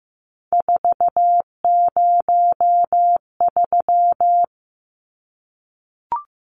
Morse code, heard as a keyed tone, about 15 wpm, 700 Hz. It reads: 403
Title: Numbers - 3-Digit 15wpm